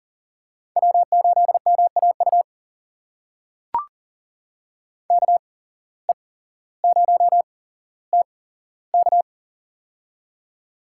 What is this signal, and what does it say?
Morse code, heard as a keyed tone, about 40 wpm, spaced as for 17 wpm, 700 Hz. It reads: W8MAU KE0TK
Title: USA Call Signs - 5 Characters - 8x Character Spacing - 40wpm